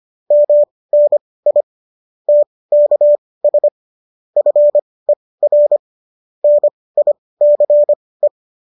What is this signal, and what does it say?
Morse code, heard as a keyed tone, about 25 wpm, spaced as for 18 wpm, 590 Hz. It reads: MNI TKS FER NICE